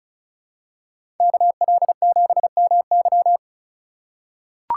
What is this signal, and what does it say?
Morse code, heard as a keyed tone, about 34 wpm, 700 Hz. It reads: KL7MY
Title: USA Call Signs - 5 Characters - 3x Character Spacing - 35wpm